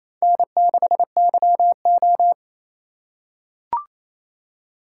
Morse code, heard as a keyed tone, about 28 wpm, 700 Hz. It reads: N6YO
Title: USA Call Signs - 4 Characters - 8x Character Spacing - 28wpm